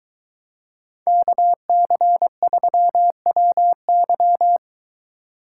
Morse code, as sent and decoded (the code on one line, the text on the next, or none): -.- -.-. ...-- .-- -.--
KC3WY